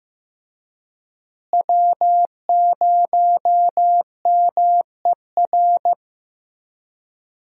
Morse code, heard as a keyed tone, about 15 wpm, 700 Hz. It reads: W0MER
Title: USA Call Signs - 5 Characters - Repeated 3x - 15wpm